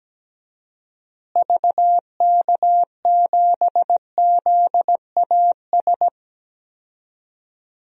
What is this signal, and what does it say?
Morse code, heard as a keyed tone, about 17 wpm, 700 Hz. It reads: VK7ZAS
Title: DXCC-100 Call Signs - 6 Characters - Repeated 2x - 17wpm